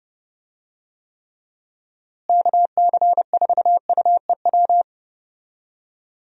Morse code, heard as a keyed tone, about 32 wpm, 700 Hz. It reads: KC4UEW
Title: USA Call Signs - 6 Characters - Repeated 2x - 30wpm